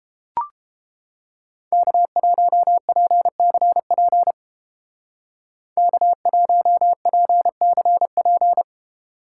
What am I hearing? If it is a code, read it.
Morse code, heard as a keyed tone, about 29 wpm, 700 Hz. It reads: K1PCP K1PCP